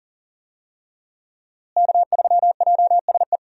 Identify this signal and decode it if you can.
Morse code, heard as a keyed tone, about 40 wpm, 700 Hz. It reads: K3JSE
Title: USA Call Signs - 5 Characters - Repeated 3x - 40wpm